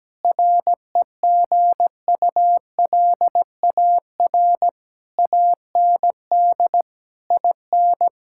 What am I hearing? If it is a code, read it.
Morse code, heard as a keyed tone, about 17 wpm, 700 Hz. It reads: REGULAR AND IN
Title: Sentences from Top 1000 Words - Review Part 6 - 17wpm